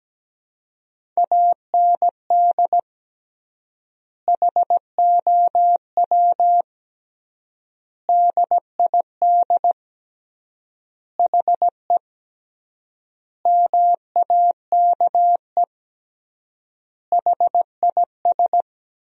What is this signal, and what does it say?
Morse code, heard as a keyed tone, about 17 wpm, 700 Hz. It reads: AND HOW DID HE MAKE HIS
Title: 3x Word-Spacing - Sentences from Top 400 Words - 17wpm